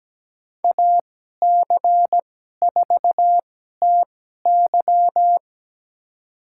Morse code, heard as a keyed tone, about 17 wpm, spaced as for 12 wpm, 700 Hz. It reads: AC4TY
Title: USA Call Signs - 5 Characters - 2x Character Spacing - 17wpm